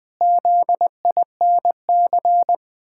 Morse code, heard as a keyed tone, about 21 wpm, 700 Hz. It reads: ZINC
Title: Pangram Challenge v2 - 20wpm